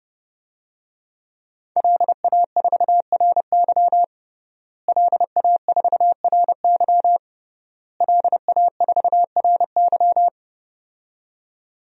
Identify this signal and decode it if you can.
Morse code, heard as a keyed tone, about 30 wpm, 700 Hz. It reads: LA4RY LA4RY LA4RY
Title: DXCC-100 Call Signs - 5 Characters - Repeated 3x - 30wpm